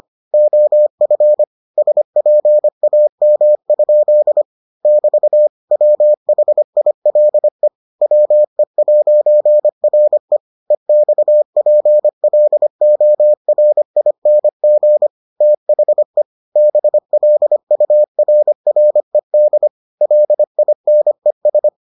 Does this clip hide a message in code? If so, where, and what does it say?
Yes: Morse code from 0.3 to 21.7 s, heard as a keyed tone, about 25 wpm, 600 Hz. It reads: OF SPAM? <BT> WHILE WE'RE EXPLORING THE BLURRED LINES